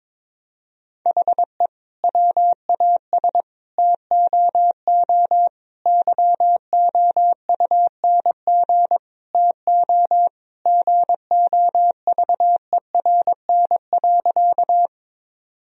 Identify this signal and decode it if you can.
Morse code, heard as a keyed tone, about 22 wpm, 700 Hz. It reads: HE WAS TOO YOUNG TO GOVERN.
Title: Sentences from Top 1000 Words - Review Part 2 - 22wpm